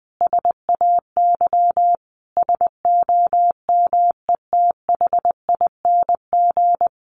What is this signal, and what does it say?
Morse code, heard as a keyed tone, about 20 wpm, 700 Hz. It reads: SAY SOMETHING
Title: Sentences from Top 1500 Words 20wpm